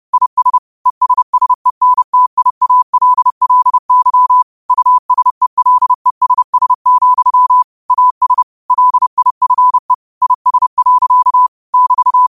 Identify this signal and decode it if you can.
Morse code, heard as a keyed tone, about 30 wpm, 1000 Hz. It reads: IS ESSENTIALLY USELESS, AS LIFE IS. <BT>
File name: qotdfiles / 30wpm